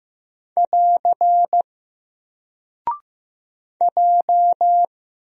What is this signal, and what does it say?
Morse code, heard as a keyed tone, about 15 wpm, 700 Hz. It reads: <AR> J
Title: Single Character (Letter:Number:Punctuation:Prosign) 15wpm